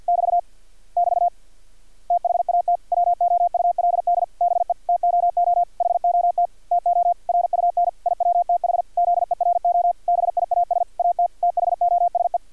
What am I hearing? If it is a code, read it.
Morse code, heard as a keyed tone, about 50 wpm, 700 Hz, with radio noise. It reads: <BT> <BT> THAT WOULD BE TOO HOT TO RUN IWTH ZERO BIAS AT THOSE
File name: archived / CRSnet